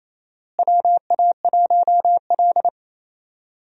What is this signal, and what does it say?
Morse code, heard as a keyed tone, about 27 wpm, 700 Hz. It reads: WA1L
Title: USA Call Signs - 4 Characters - Repeated 3x - 28wpm